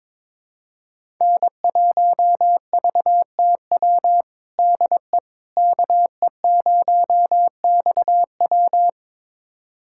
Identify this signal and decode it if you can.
Morse code, heard as a keyed tone, about 22 wpm, 700 Hz. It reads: N1VTW DE KE0XW
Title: USA Callsign DE Callsign - 5 Characters - 22wpm